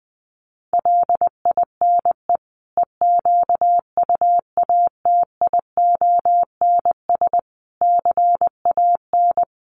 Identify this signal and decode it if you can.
Morse code, heard as a keyed tone, about 20 wpm, 700 Hz. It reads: LINE EQUATIONS CAN
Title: Sets of 5 Words 20wpm